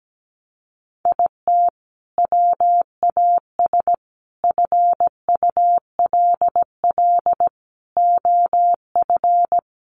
Morse code, heard as a keyed tone, about 17 wpm, 700 Hz. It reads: IT WAS FULL OF
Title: Sentences from Top 500 Words 17wpm